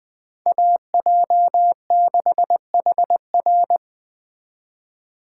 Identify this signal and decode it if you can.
Morse code, heard as a keyed tone, about 20 wpm, 700 Hz. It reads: AJ6HR